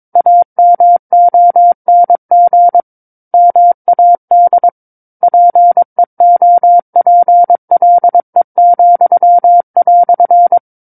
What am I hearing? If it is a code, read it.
Morse code, heard as a keyed tone, about 22 wpm, 700 Hz. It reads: AMONG MAD PEOPLE,"
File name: 06-Chapter-06 22 WPM